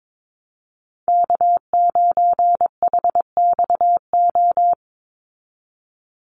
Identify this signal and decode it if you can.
Morse code, heard as a keyed tone, about 22 wpm, 700 Hz. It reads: K9HXO
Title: Call Signs Repeated 2x - 22wpm